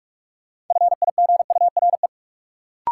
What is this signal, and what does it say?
Morse code, heard as a keyed tone, about 46 wpm, 700 Hz. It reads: FIGURE
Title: Sentences from Top 1400 Words - New Word Drill Alt - 45wpm